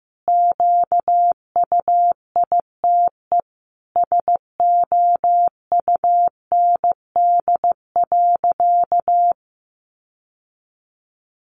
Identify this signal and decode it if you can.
Morse code, heard as a keyed tone, about 14 wpm, 700 Hz. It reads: QUITE SOUND.
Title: Sentences from Top 1200 Words 15wpm